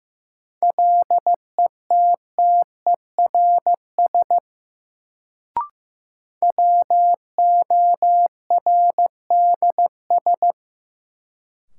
Morse code, heard as a keyed tone, about 15 wpm, 700 Hz. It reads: LETTERS WORDS
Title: Sentences from Top 1000 Words - Review Part 6 - New Word Drill - 15wpm